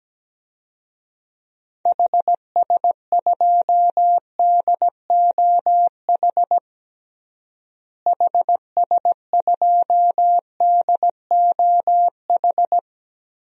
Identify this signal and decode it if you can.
Morse code, heard as a keyed tone, about 17 wpm, 700 Hz. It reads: HS2DOH HS2DOH
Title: DXCC-100 Call Signs - 6 Characters - Repeated 2x - 17wpm